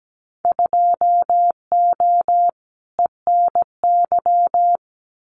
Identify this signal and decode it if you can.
Morse code, heard as a keyed tone, about 17 wpm, 700 Hz. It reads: 2O ENY
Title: Winter Field Day Exchanges 17wpm